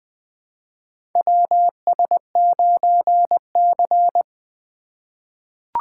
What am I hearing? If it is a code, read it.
Morse code, heard as a keyed tone, about 20 wpm, 700 Hz. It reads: WS9C
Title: USA Call Signs - 4 Characters - 2x Character Spacing - 20wpm